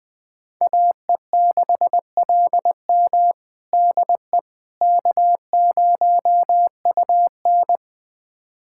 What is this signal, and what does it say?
Morse code, heard as a keyed tone, about 20 wpm, 700 Hz. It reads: AE6LM DE K0UN
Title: USA Callsign DE Callsign 20wpm